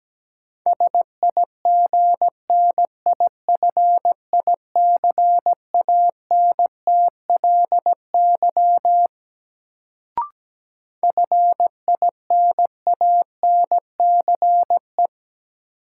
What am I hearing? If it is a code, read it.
Morse code, heard as a keyed tone, about 17 wpm, 700 Hz. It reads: SIGNIFICANTLY FINANCE
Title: Sentences from Top 1500 Words - New Word Drill - 17wpm